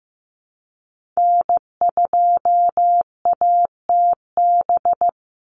Morse code, heard as a keyed tone, about 15 wpm, 700 Hz. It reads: N2ATB
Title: Call Signs Repeated 3x - 15wpm